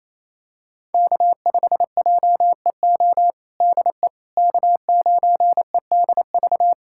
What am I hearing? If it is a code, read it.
Morse code, heard as a keyed tone, about 28 wpm, 700 Hz. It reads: K5JEO DE K9EDV
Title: USA Callsign DE Callsign - 5 Characters - 28wpm